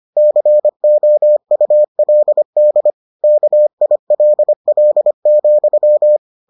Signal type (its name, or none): Morse code